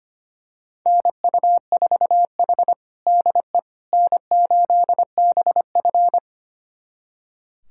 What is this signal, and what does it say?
Morse code, heard as a keyed tone, about 25 wpm, 700 Hz. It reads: NU4H DE N8BF
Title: USA Callsign DE Callsign - 4 Characters - 25wpm